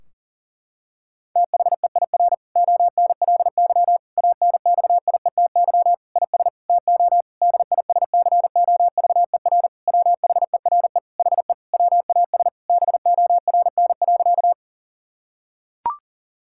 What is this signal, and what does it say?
Morse code, heard as a keyed tone, about 40 wpm, 700 Hz. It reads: THEIR ONLY ANXIETY IS TO DISCOVER WHERE HE WAS BORN.
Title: Sentences from Top 1600 Words 40wpm